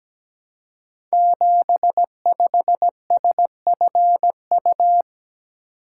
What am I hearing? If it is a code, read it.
Morse code, heard as a keyed tone, about 17 wpm, 700 Hz. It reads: 75SFU
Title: Code Groups - 5 Letter:Numbers - 17wpm